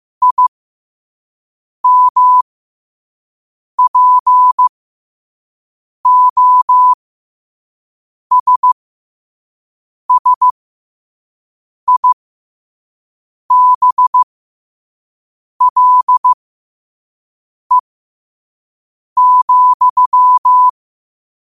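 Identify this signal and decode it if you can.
Morse code, heard as a keyed tone, about 15 wpm, spaced as for 5 wpm, 1000 Hz. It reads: IMPOSSIBLE,
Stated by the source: Morse at 5 wpm.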